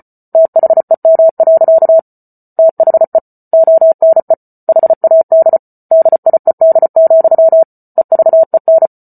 Morse code, heard as a keyed tone, about 34 wpm, 650 Hz. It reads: THEM. THE ONE HAD DIED, EVEN